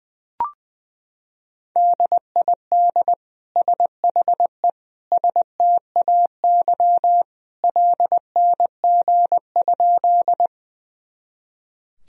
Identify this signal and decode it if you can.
Morse code, heard as a keyed tone, about 20 wpm, 700 Hz. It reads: DID SHE STAY LNG?
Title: Sentences from Top 500 Words and Abbreviations 20wpm